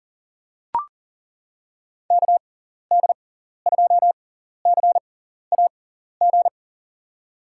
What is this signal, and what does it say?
Morse code, heard as a keyed tone, about 40 wpm, spaced as for 14 wpm, 700 Hz. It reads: KD2CAG
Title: USA Call Signs - 6 Characters - 6x Character Spacing - 40wpm